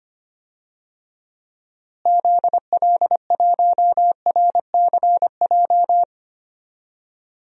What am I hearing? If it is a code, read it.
Morse code, heard as a keyed tone, about 25 wpm, 700 Hz. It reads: ZL1RCJ
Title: DXCC-100 Call Signs - 6 Characters - Repeated 2x - 25wpm